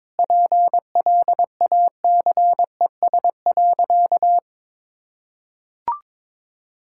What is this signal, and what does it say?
Morse code, heard as a keyed tone, about 22 wpm, 700 Hz. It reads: PLACES.
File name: practice / 22wpm / Sentences with 4-Word Phrases - 22wpm